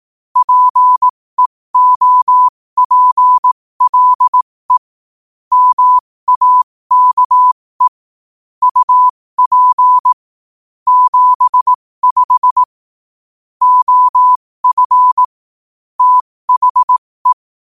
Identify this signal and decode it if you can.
Morse code, heard as a keyed tone, about 18 wpm, spaced as for 15 wpm, 1000 Hz. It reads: PEOPLE MAKE UP 75 OF THE
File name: qotdfiles / 15wpm